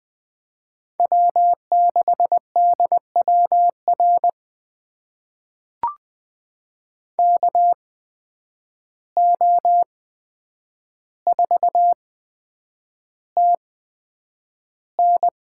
Morse code, heard as a keyed tone, about 20 wpm, spaced as for 9 wpm, 700 Hz. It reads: W6DWR KO4TN
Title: USA Call Signs - 5 Characters - 8x Character Spacing - 20wpm